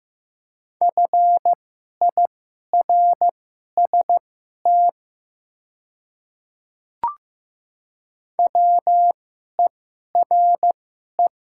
Morse code, heard as a keyed tone, about 15 wpm, spaced as for 11 wpm, 700 Hz. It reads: FIRST WERE
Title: 2x Character Spacing - Top 100 Words - 15wpm